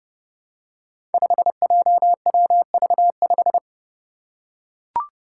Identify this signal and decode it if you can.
Morse code, heard as a keyed tone, about 29 wpm, 700 Hz. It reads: 5JWV5
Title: Five Challenging Letter-Number Combinations 30wpm